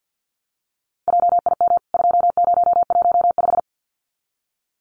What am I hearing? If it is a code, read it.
Morse code, heard as a keyed tone, about 50 wpm, 700 Hz. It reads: JIM 2015
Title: CWT Exchange 50wpm